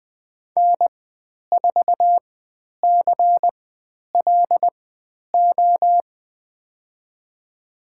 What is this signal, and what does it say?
Morse code, heard as a keyed tone, about 20 wpm, spaced as for 10 wpm, 700 Hz. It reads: N4CLO
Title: 095 Lesson 16g - Letter c - Call Signs 20wpm